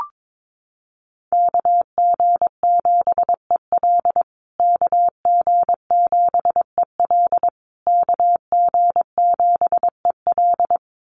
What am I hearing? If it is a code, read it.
Morse code, heard as a keyed tone, about 22 wpm, 700 Hz. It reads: KG7EL KG7EL KG7EL
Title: USA Call Signs Repeated 3x 22wpm